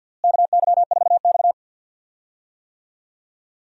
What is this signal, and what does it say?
Morse code, heard as a keyed tone, about 52 wpm, 700 Hz. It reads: KY4X
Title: USA Call Signs - 4 Characters - Repeated 3x - 50wpm